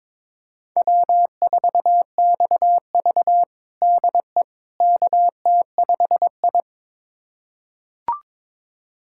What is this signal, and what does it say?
Morse code, heard as a keyed tone, about 22 wpm, 700 Hz. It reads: W4XV DE KT5I
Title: USA Callsign DE Callsign - 4 Characters - 22wpm